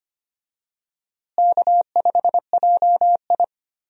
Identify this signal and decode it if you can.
Morse code, heard as a keyed tone, about 25 wpm, 700 Hz. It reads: K5JI
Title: USA Call Signs - 4 Characters - Repeated 3x - 25wpm